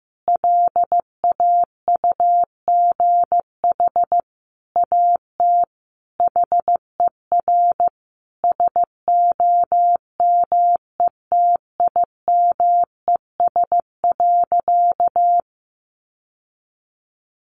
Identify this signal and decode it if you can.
Morse code, heard as a keyed tone, about 15 wpm, 700 Hz. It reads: LAUGH AT HER SOMETIMES.
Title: Sentences from Top 600 Words 15wpm